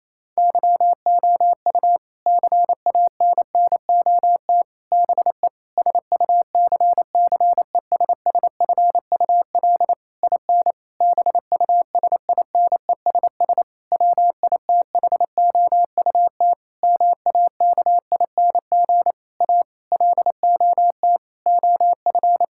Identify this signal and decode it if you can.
Morse code, heard as a keyed tone, about 28 wpm, 700 Hz. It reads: YOU CANNOT BE SUCCESSFUL IN BUSINESS WITHOUT MAKING A LOT OF
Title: Sentences from Top 1000 Words - Review Part 6 - 28wpm